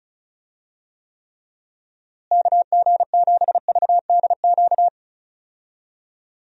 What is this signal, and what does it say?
Morse code, heard as a keyed tone, about 36 wpm, 700 Hz. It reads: KG7VDQ
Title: USA Call Signs - 6 Characters - Repeated 2x - 35wpm